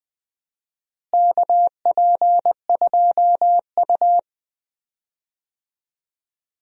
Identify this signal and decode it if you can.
Morse code, heard as a keyed tone, about 20 wpm, 700 Hz. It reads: KP2U
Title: DXCC-100 Call Signs - 4 Characters - Repeated 2x - 20wpm